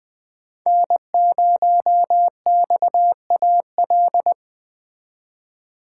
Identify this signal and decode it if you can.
Morse code, heard as a keyed tone, about 20 wpm, 700 Hz. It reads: N0XAL